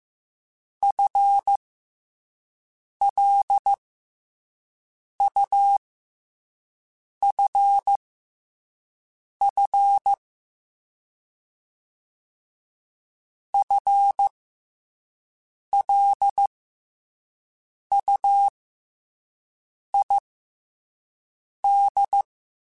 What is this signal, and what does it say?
Morse code, heard as a keyed tone, about 15 wpm, spaced as for 5 wpm, 800 Hz. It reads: FLUFF FLUID
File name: FL Runs / FL Runs mp3s / FL Runs @5